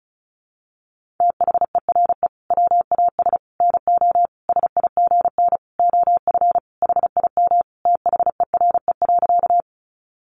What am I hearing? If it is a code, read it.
Morse code, heard as a keyed tone, about 34 wpm, 700 Hz. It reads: THERE WAS NO SIGN OF HIM THERE.
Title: Sentences from Top 1000 Words 35wpm